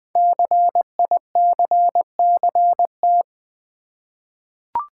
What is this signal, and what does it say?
Morse code, heard as a keyed tone, about 20 wpm, 700 Hz. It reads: CICCT
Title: Code Groups - 5 Letters - 20wpm